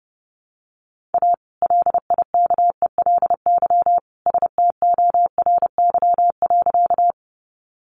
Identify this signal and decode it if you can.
Morse code, heard as a keyed tone, about 30 wpm, 700 Hz. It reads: A LIKELY STORY.
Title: Sentences from Top 600 Words 30wpm